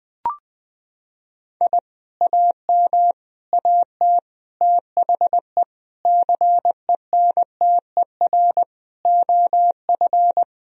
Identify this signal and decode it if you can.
Morse code, heard as a keyed tone, about 20 wpm, 700 Hz. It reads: I AM AT THE CENTER OF
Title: Sentences from Top 1000 Words - Review Part 2 - 20wpm